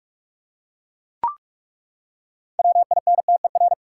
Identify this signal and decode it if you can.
Morse code, heard as a keyed tone, about 42 wpm, 700 Hz. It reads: WINTER
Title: Six Letter Words - 45wpm